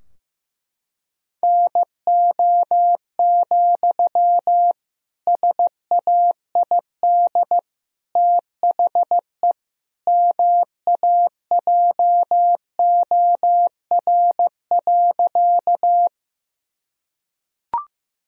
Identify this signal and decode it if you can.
Morse code, heard as a keyed tone, about 15 wpm, 700 Hz. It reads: NO, SAID THE MAJOR.
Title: Sentences from Top 1000 Words - Review Part 4 - 15wpm